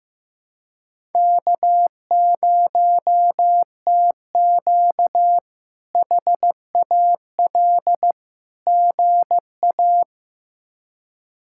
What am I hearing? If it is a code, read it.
Morse code, heard as a keyed tone, about 15 wpm, 700 Hz. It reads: K0TQ HAL GA
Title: K1USN SST QSO Practice - 15wpm